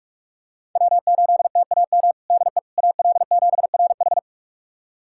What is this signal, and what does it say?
Morse code, heard as a keyed tone, about 43 wpm, 700 Hz. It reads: W8TAM DE AL7RH
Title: USA Callsign DE Callsign - 5 Characters - 45wpm